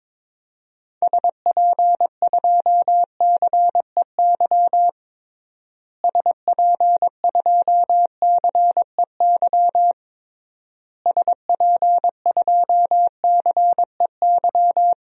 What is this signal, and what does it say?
Morse code, heard as a keyed tone, about 22 wpm, 700 Hz. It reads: SP2CEY SP2CEY SP2CEY
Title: DXCC-100 Call Signs - 6 Characters - Repeated 3x - 22wpm